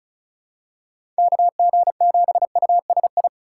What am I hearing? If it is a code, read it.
Morse code, heard as a keyed tone, about 36 wpm, 700 Hz. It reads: KG7USI
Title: USA Call Signs - 6 Characters - 2x Character Spacing - 35wpm